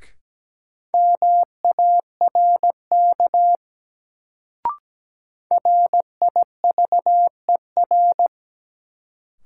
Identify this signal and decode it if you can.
Morse code, heard as a keyed tone, about 17 wpm, 700 Hz. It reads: MARK RIVER
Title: Sentences from Top 300 Words - New Word Drill - 17wpm